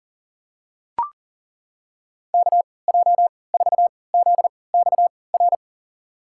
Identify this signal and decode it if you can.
Morse code, heard as a keyed tone, about 40 wpm, spaced as for 23 wpm, 700 Hz. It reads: KJ4ZXR
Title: USA Call Signs - 6 Characters - 3x Character Spacing - 40wpm